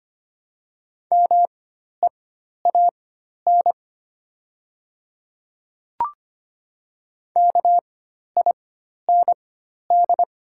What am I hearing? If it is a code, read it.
Morse code, heard as a keyed tone, about 25 wpm, spaced as for 12 wpm, 700 Hz. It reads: MEAN KIND
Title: 4x Character Spacing - Top 300 Words - 25wpm